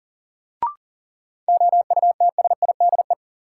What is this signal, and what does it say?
Morse code, heard as a keyed tone, about 40 wpm, 700 Hz. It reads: OUTSIDE
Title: Sentences from Top 900 Words - New Word Drill - 40wpm